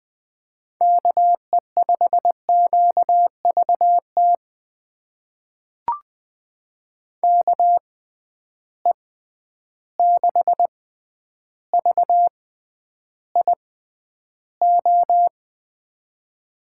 Morse code, heard as a keyed tone, about 20 wpm, spaced as for 10 wpm, 700 Hz. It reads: KE5QVT KE6VIO
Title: USA Call Signs - 6 Characters - 6x Character Spacing - 20wpm